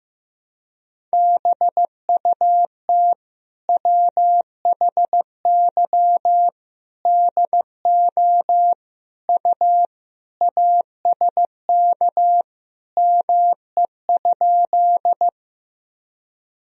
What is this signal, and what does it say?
Morse code, heard as a keyed tone, about 15 wpm, 700 Hz. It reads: BUT WHY DO U ASK ME?
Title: Sentences from Top 200 Words and Abbreviations 15wpm